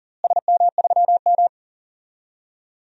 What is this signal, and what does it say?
Morse code, heard as a keyed tone, about 39 wpm, 700 Hz. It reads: SM3M
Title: DXCC-100 Call Signs - 4 Characters - Repeated 2x - 40wpm